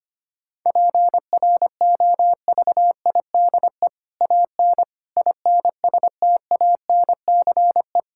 Morse code, heard as a keyed tone, about 25 wpm, 700 Hz. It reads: PROVIDE AN INSTANCE
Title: Sentences from Top 1000 Words - Review Part 5 - 25wpm